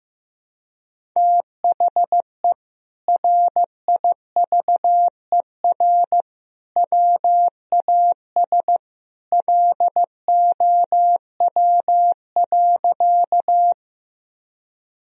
Morse code, heard as a keyed tone, about 16 wpm, 700 Hz. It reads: THE RIVER WAS LOW.
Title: Sentences from Top 1000 Words - Review Part 1 - 15wpm